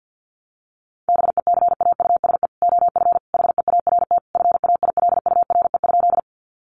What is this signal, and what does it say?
Morse code, heard as a keyed tone, about 50 wpm, 700 Hz. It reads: BECAUSE OF HEART FAILURE?
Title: Sentences from Top 1200 Words 50wpm